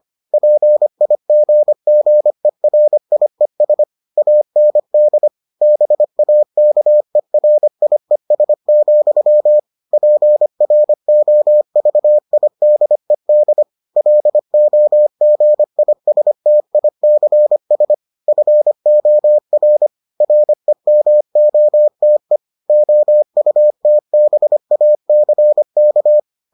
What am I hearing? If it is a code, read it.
Morse code, heard as a keyed tone, about 25 wpm, 600 Hz. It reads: PIGGERIES AND BAKERIES, PROVIDED LOGISTICS FOR REMOTE OUTBACK